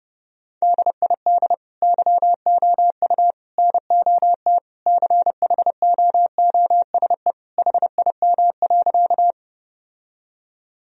Morse code, heard as a keyed tone, about 30 wpm, 700 Hz. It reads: DID YOU NOT CHOOSE HIM.
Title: Sentences from Top 1000 Words - Review Part 4 - 30wpm